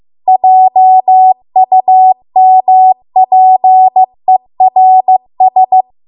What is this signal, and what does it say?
Morse code, heard as a keyed tone, about 15 wpm, 750 Hz. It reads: JUMPERS